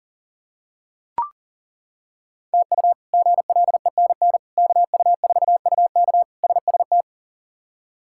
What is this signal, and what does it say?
Morse code, heard as a keyed tone, about 40 wpm, 700 Hz. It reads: TU GLENN KU4UK SST
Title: K1USN SST QSO Practice - 40wpm